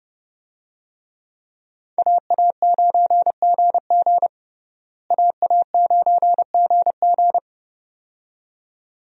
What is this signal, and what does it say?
Morse code, heard as a keyed tone, about 30 wpm, 700 Hz. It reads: AA9GG AA9GG
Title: USA Call Signs - 5 Characters - Repeated 2x - 30wpm